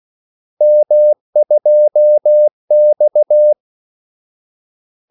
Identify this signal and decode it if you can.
Morse code, heard as a keyed tone, about 16 wpm, 600 Hz. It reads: M2X
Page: Kaldesignaler 16 wpm